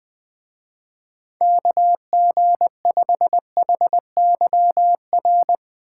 Morse code, heard as a keyed tone, about 20 wpm, 700 Hz. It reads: KG5HYR
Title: Speed Racing - USA 2x3 Call Signs 20wpm